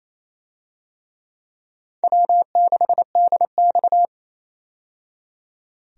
Morse code, heard as a keyed tone, about 28 wpm, 700 Hz. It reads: W6DX
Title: USA Call Signs - 4 Characters - 28wpm